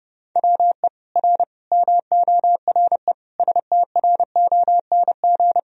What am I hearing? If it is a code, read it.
Morse code, heard as a keyed tone, about 30 wpm, 700 Hz. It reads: WE R MORE STRONG
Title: Sentences from Top 500 Words and Abbreviations 30wpm